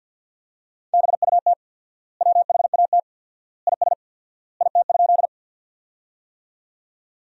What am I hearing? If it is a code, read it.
Morse code, heard as a keyed tone, about 50 wpm, 700 Hz. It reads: BUT WHAT IS IT?
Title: 4x Word-Spacing - Sentences from Top 100 Words - 50wpm